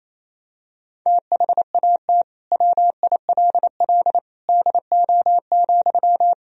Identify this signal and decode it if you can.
Morse code, heard as a keyed tone, about 28 wpm, 700 Hz. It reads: THAT WILL DO,
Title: Sentences from Top 100 Words and Abbreviations 28wpm